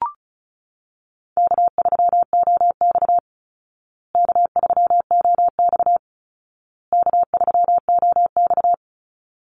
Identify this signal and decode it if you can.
Morse code, heard as a keyed tone, about 34 wpm, 700 Hz. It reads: K3OX K3OX K3OX
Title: Easy Call Signs Repeated 3x - 35wpm